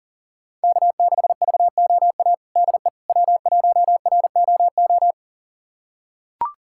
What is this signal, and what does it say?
Morse code, heard as a keyed tone, about 40 wpm, 700 Hz. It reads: K6VOA DE W1ROO
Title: USA Callsign DE Callsign - 5 Characters - 40wpm